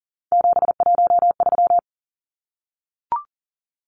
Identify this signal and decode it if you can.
Morse code, heard as a keyed tone, about 40 wpm, 700 Hz. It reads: 713
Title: Numbers - 3-Digit 40wpm